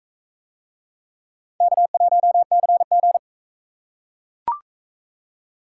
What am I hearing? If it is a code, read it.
Morse code, heard as a keyed tone, about 41 wpm, 700 Hz. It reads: K1CG